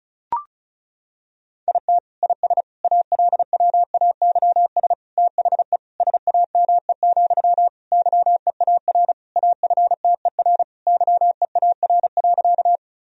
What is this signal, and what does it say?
Morse code, heard as a keyed tone, about 34 wpm, 690 Hz. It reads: IT IS ALWAYS THE SAME, YEAR AFTER YEAR.
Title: Sentences from Top 1000 Words - Review Part 1 - 35wpm